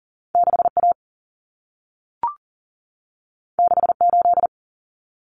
Morse code, heard as a keyed tone, about 40 wpm, 700 Hz. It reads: 6A 68
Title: Common Cut Numbers - 2-Digit 40wpm